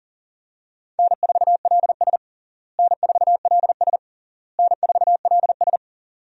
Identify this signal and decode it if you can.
Morse code, heard as a keyed tone, about 37 wpm, 700 Hz. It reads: N4LS N4LS N4LS
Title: USA Call Signs - 4 Characters - Repeated 3x - 40wpm